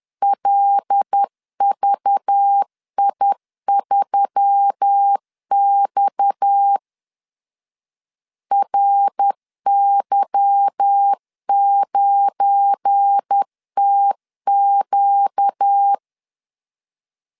Morse code, heard as a keyed tone, about 10 wpm, 800 Hz. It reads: LVI3X RY9TQ
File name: LEZIONI / 8 YJ45 / 8 Y 10wpm 2615